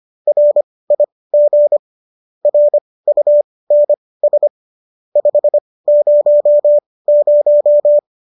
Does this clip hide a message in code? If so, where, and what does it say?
Yes: Morse code from 0.3 to 8.0 s, heard as a keyed tone, about 25 wpm, spaced as for 18 wpm, 590 Hz. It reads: RIG RUNS 500